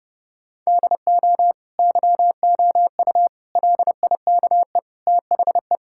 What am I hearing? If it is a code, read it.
Morse code, heard as a keyed tone, about 30 wpm, 700 Hz. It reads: DO YOU LIKE THE
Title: Sentences from Top 1000 Words - Review Part 5 - 30wpm